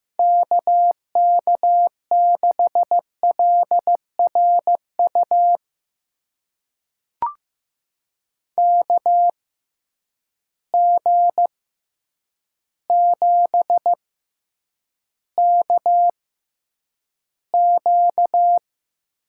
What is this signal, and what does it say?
Morse code, heard as a keyed tone, about 15 wpm, spaced as for 8 wpm, 700 Hz. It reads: KK6LRU KG7KQ
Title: USA Call Signs - 6 Characters - 6x Character Spacing - 15wpm